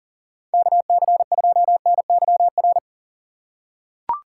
Morse code, heard as a keyed tone, about 40 wpm, 700 Hz. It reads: KC2NYR
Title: USA Call Signs - 6 Characters - 3x Character Spacing - 40wpm